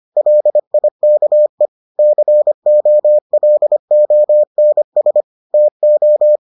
Morse code, heard as a keyed tone, about 25 wpm, 600 Hz. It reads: LIKE COLONS TO